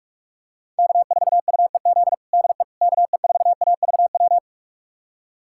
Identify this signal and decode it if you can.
Morse code, heard as a keyed tone, about 44 wpm, 700 Hz. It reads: K4UEZ DE KE4AVW
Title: USA Callsign DE Callsign 45wpm